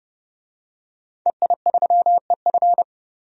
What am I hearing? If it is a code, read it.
Morse code, heard as a keyed tone, about 30 wpm, 700 Hz. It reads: EI3EF
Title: DXCC-100 Call Signs - 5 Characters - 30wpm